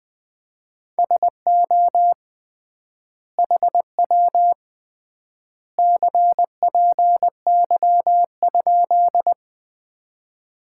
Morse code, heard as a keyed tone, about 21 wpm, 700 Hz. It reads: SO HW CPY?
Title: Common QSO Phrases - 3x Word-Spacing - 20wpm